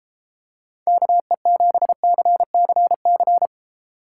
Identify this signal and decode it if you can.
Morse code, heard as a keyed tone, about 31 wpm, 700 Hz. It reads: KE7CCC